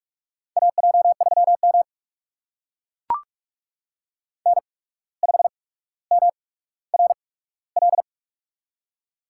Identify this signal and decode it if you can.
Morse code, heard as a keyed tone, about 44 wpm, spaced as for 18 wpm, 690 Hz. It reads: AJ3M N5MRL
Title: USA Call Signs - 8x Character Spacing - 45wpm